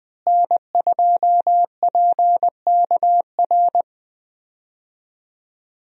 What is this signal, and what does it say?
Morse code, heard as a keyed tone, about 20 wpm, 700 Hz. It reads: N2PKR
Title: USA Call Signs - 5 Characters - Repeated 3x - 20wpm